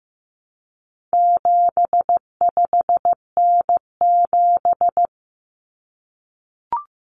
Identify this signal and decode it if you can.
Morse code, heard as a keyed tone, about 15 wpm, 700 Hz. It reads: 75N7
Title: Common Cut Numbers - 4-Digit 15wpm